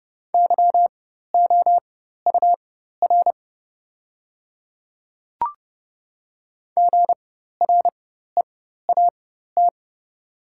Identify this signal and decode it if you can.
Morse code, heard as a keyed tone, about 30 wpm, spaced as for 14 wpm, 700 Hz. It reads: YOUR GREAT
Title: 4x Character Spacing - Top 300 Words - 30wpm